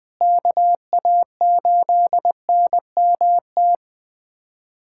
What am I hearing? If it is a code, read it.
Morse code, heard as a keyed tone, about 20 wpm, 700 Hz. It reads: KA8NMT
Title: USA Callsign DE Callsign - 6 Characters - 20wpm